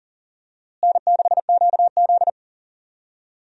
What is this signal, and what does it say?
Morse code, heard as a keyed tone, about 40 wpm, 700 Hz. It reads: N6QZ
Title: USA Call Signs - 4 Characters - Repeated 2x - 40wpm